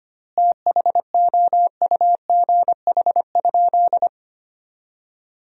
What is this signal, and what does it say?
Morse code, heard as a keyed tone, about 25 wpm, 700 Hz. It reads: THOUGH?
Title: 4x Word-Spacing - Sentences from Top 400 Words - 25wpm